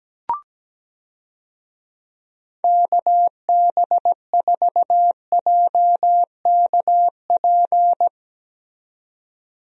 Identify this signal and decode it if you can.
Morse code, heard as a keyed tone, about 17 wpm, 700 Hz. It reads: KB4JKP
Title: USA Call Signs - 6 Characters - 17wpm